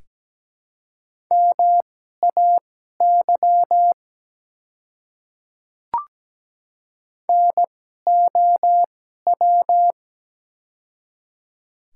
Morse code, heard as a keyed tone, about 17 wpm, spaced as for 12 wpm, 700 Hz. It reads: MAY NOW
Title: 2x Character Spacing - Top 200 Words - 17wpm